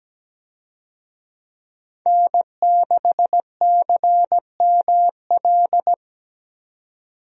Morse code, heard as a keyed tone, about 17 wpm, 700 Hz. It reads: N6CML
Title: USA Call Signs - 5 Characters - 17wpm